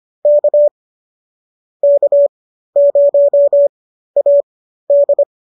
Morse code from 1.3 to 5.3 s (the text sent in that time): K0AD